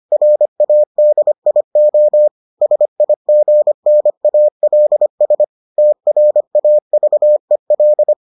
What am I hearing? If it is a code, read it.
Morse code, heard as a keyed tone, about 25 wpm, 600 Hz. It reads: RADIO SIGNALS TRAVEL